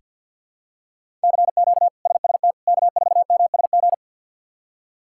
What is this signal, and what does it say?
Morse code, heard as a keyed tone, about 49 wpm, 700 Hz. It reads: CQ SST K4MSG
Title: K1USN SST QSO Practice - 50wpm